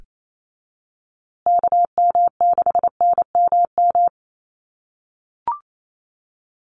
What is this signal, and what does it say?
Morse code, heard as a keyed tone, about 28 wpm, 700 Hz. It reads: KM6NMM
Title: USA Call Signs Repeated 2x 28wpm